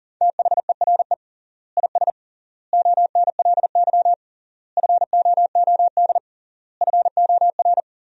Morse code, heard as a keyed tone, about 37 wpm, 700 Hz. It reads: THERE IS ONLY FOOD FOR
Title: 3x Word-Spacing - Sentences from Top 300 Words - 40wpm